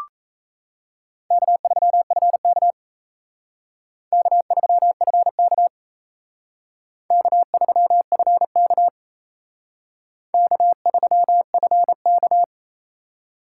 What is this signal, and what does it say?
Morse code, heard as a keyed tone, about 35 wpm, 700 Hz. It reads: K3FK K3FK K3FK K3FK
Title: Speed Racing - USA 1x2 Call Signs 28wpm